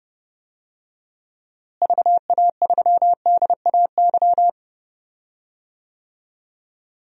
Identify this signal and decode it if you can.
Morse code, heard as a keyed tone, about 31 wpm, 700 Hz. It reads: VA3DAY
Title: DXCC-100 Call Signs - 6 Characters - Repeated 3x - 30wpm